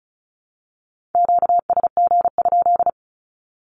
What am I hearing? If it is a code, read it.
Morse code, heard as a keyed tone, about 34 wpm, 700 Hz. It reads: QSG?
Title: Q-Codes 35wpm